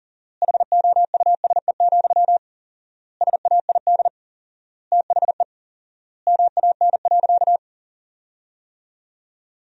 Morse code, heard as a keyed tone, about 40 wpm, 700 Hz. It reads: HOUSE, SAID THE MAN.
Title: 4x Word-Spacing - Sentences from Top 400 Words - 40wpm